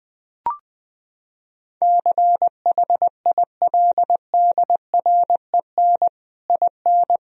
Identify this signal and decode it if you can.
Morse code, heard as a keyed tone, about 21 wpm, 700 Hz. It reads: CHILDREN IN
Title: Sentences from Top 1000 Words - Review Part 4 - 20wpm